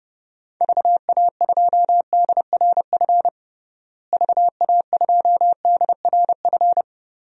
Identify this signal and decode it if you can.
Morse code, heard as a keyed tone, about 30 wpm, 700 Hz. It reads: VA2DRF VA2DRF
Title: DXCC-100 Call Signs - 6 Characters - Repeated 3x - 30wpm